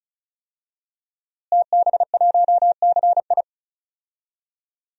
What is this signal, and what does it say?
Morse code, heard as a keyed tone, about 34 wpm, 690 Hz. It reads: TB1CI
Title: DXCC-100 Call Signs - 5 Characters - Repeated 3x - 35wpm